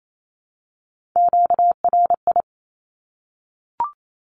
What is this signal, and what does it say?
Morse code, heard as a keyed tone, about 28 wpm, 700 Hz. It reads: QRI
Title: Q-Codes Repeated 3x 28wpm